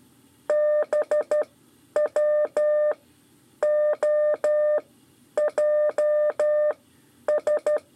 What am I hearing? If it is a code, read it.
Morse code, heard as a keyed tone, about 12 wpm, spaced as for 9 wpm, 600 Hz. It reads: BWOJS
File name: 5 LEZIONE RIPASSO 2